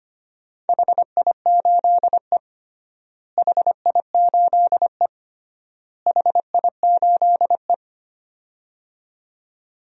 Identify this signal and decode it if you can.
Morse code, heard as a keyed tone, about 25 wpm, 700 Hz. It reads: HI8E HI8E HI8E
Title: DXCC-100 Call Signs - 4 Characters - Repeated 3x - 25wpm